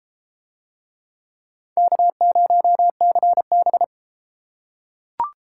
Morse code, heard as a keyed tone, about 32 wpm, 700 Hz. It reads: K0CB